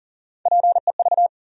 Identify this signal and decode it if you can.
Morse code, heard as a keyed tone, about 40 wpm, 700 Hz. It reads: PEV
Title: USA Call Sign Postfixes - 3 Letters 40wpm